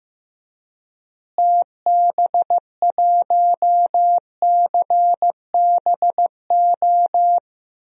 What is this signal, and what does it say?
Morse code, heard as a keyed tone, about 15 wpm, 700 Hz. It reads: TB1CBO
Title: DXCC-100 Call Signs - 6 Characters - Repeated 3x - 15wpm